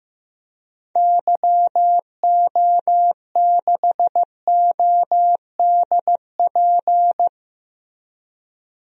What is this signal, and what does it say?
Morse code, heard as a keyed tone, about 15 wpm, 700 Hz. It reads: YO6ODP
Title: DXCC-100 Call Signs - 6 Characters - Repeated 2x - 15wpm